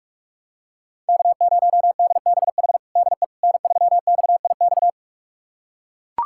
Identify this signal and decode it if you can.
Morse code, heard as a keyed tone, about 41 wpm, 700 Hz. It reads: K0DBH DE N3XIX
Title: USA Callsign DE Callsign - 5 Characters - 45wpm